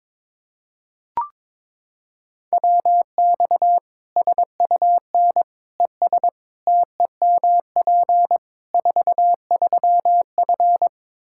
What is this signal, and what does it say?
Morse code, heard as a keyed tone, about 22 wpm, 700 Hz. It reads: WX SUN ES TEMP 43F
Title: Practice QSO - Exchange 2 - Info - 22wpm